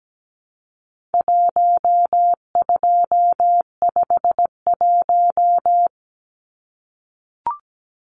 Morse code, heard as a keyed tone, about 17 wpm, 700 Hz. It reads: 1251
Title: Numbers - Contest Style - 4-Digit 17wpm